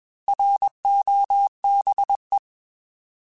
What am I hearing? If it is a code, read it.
Morse code, heard as a keyed tone, about 20 wpm, 800 Hz. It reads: ROBE
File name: Alphabetical / Alphabetical mp3s / Rs @20